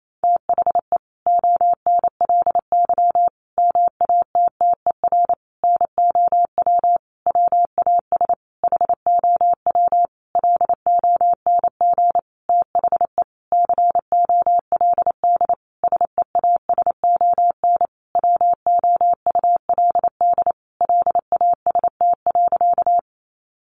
Morse code, heard as a keyed tone, about 28 wpm, 700 Hz. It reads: THE ONLY MATTER NOW WAS HOW LONG THE COLD SEASON WOULD LAST.
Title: Sentences from Top 1400 Words 28wpm